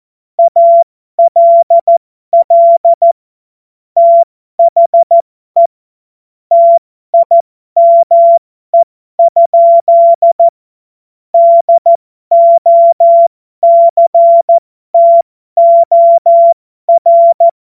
Morse code, heard as a keyed tone, about 14 wpm, spaced as for 12 wpm, 680 Hz. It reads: ALL THE TIME? DOCTOR